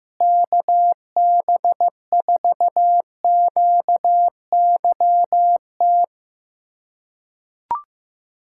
Morse code, heard as a keyed tone, about 15 wpm, 700 Hz. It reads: KB4QYT
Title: USA Callsign DE Callsign - 6 Characters - 15wpm